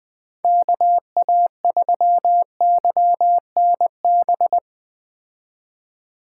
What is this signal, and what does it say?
Morse code, heard as a keyed tone, about 20 wpm, 700 Hz. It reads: KA3YNB